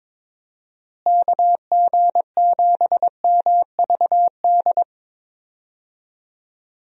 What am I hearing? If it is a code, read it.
Morse code, heard as a keyed tone, about 22 wpm, 700 Hz. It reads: KG7MVD
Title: USA Call Signs - 6 Characters - Repeated 2x - 22wpm